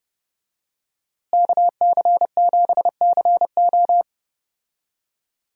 Morse code, heard as a keyed tone, about 31 wpm, 700 Hz. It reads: KC7CO
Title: USA Call Signs - 5 Characters - Repeated 2x - 30wpm